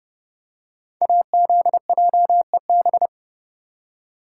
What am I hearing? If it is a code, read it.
Morse code, heard as a keyed tone, about 30 wpm, 700 Hz. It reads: AZJEB
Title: Code Groups - 5 Letter:Numbers - 30wpm